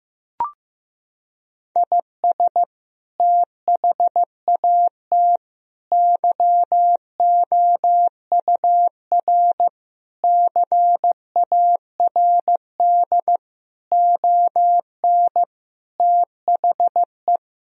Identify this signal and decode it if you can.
Morse code, heard as a keyed tone, about 15 wpm, 700 Hz. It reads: IS THAT YOUR CARD ON THE
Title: Sentences from Top 1000 Words - Review Part 4 - 15wpm